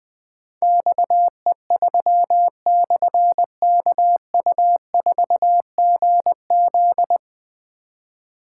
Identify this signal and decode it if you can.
Morse code, heard as a keyed tone, about 20 wpm, 700 Hz. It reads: XE3/KU4GZ
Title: Difficult Call Signs 20wpm